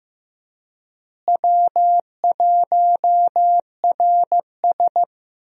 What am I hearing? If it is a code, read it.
Morse code, heard as a keyed tone, about 15 wpm, 700 Hz. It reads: W1RS
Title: USA Call Signs - 4 Characters - Repeated 3x - 15wpm